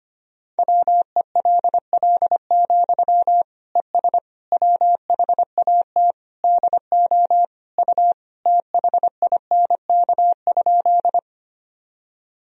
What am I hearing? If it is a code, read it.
Morse code, heard as a keyed tone, about 25 wpm, 700 Hz. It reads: WELL, ES WHAT DO U THINK?
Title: Sentences from Top 200 Words and Abbreviations 25wpm